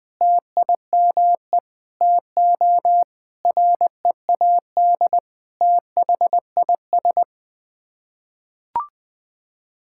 Morse code, heard as a keyed tone, about 21 wpm, 700 Hz. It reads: TIME TO READ THIS
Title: Sets of 9 Words - 20wpm